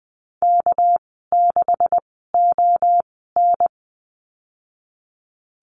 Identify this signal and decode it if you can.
Morse code, heard as a keyed tone, about 20 wpm, spaced as for 14 wpm, 700 Hz. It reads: K6ON
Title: 245 Lesson 39g - Letter z - Call Signs 20wpm